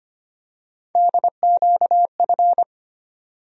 Morse code, heard as a keyed tone, about 25 wpm, 700 Hz. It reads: DQF
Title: USA Call Sign Postfixes - 3 Letters 25wpm